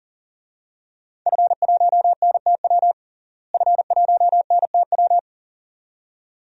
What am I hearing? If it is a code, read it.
Morse code, heard as a keyed tone, about 40 wpm, 700 Hz. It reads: F1NTW F1NTW
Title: DXCC-100 Call Signs - 5 Characters - Repeated 2x - 40wpm